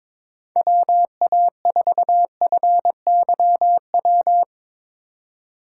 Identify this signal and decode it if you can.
Morse code, heard as a keyed tone, about 22 wpm, 700 Hz. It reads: WA4FYW
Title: USA Call Signs - 6 Characters - 4x Character Spacing - 22wpm